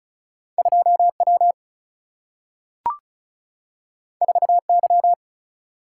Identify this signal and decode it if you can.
Morse code, heard as a keyed tone, about 36 wpm, 700 Hz. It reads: 2 W 4 Y
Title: Single-Letter Single-Number 35wpm